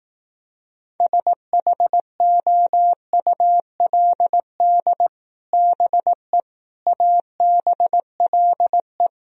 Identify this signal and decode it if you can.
Morse code, heard as a keyed tone, about 18 wpm, 700 Hz. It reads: SHOULD BE ABLE
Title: Speed-Racing - Sets of 4 Words 15wpm